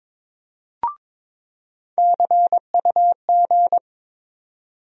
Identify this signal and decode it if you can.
Morse code, heard as a keyed tone, about 22 wpm, 700 Hz. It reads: CUG
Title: USA Call Sign Postfixes - 3 Letters 22wpm